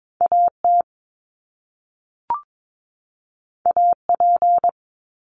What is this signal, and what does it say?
Morse code, heard as a keyed tone, about 22 wpm, 700 Hz. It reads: AT AP
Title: Top Two Letter Combinations in Words 22wpm